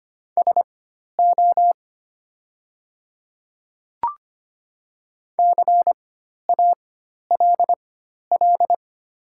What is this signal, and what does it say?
Morse code, heard as a keyed tone, about 25 wpm, spaced as for 12 wpm, 700 Hz. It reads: SO CALL
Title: 4x Character Spacing - Top 200 Words - 25wpm